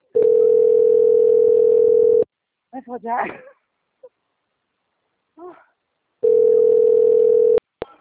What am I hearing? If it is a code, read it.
Ringback tone (North America), 440 + 480 Hz, with speech.
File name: out- - - - - .43999